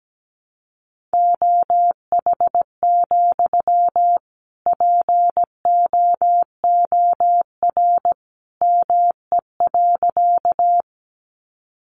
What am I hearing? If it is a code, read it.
Morse code, heard as a keyed tone, about 17 wpm, 700 Hz. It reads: OH, POOR ME.
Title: Sentences from Top 1000 Words 17wpm